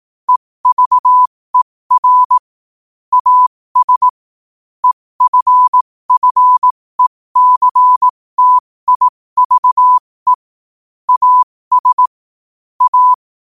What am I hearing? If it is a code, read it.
Morse code, heard as a keyed tone, about 18 wpm, spaced as for 15 wpm, 1000 Hz. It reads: EVER AS EFFECTIVE AS A
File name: qotdfiles / 15wpm